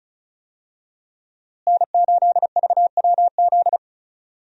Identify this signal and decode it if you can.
Morse code, heard as a keyed tone, about 36 wpm, 700 Hz. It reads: N8VWZ
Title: USA Call Signs - 5 Characters - Repeated 3x - 35wpm